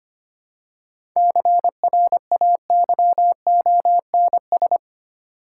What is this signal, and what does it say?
Morse code, heard as a keyed tone, about 26 wpm, 700 Hz. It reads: CRAYONS
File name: practice / 25wpm / Easy Words with Hints - 25wpm